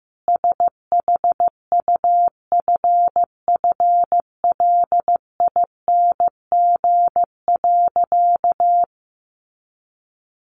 Morse code, heard as a keyed tone, about 15 wpm, 700 Hz. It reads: SHUFFLING.
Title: A Sound Of Thunder 15wpm